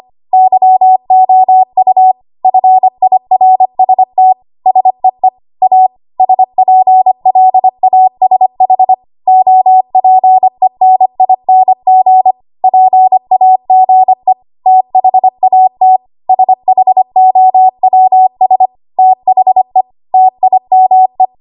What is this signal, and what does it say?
Morse code, heard as a keyed tone, about 25 wpm, 750 Hz. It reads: YOU FIRST SEE A SPLASH OPENING PAGE THAT SHOWS THE TIME
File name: Archive / 25 WPM / 25WPM